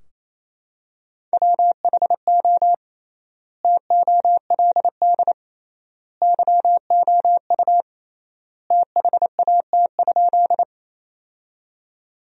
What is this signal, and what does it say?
Morse code, heard as a keyed tone, about 28 wpm, 700 Hz. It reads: WHO TOLD YOU THAT?
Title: 3x Word-Spacing - Sentences from Top 400 Words - 28wpm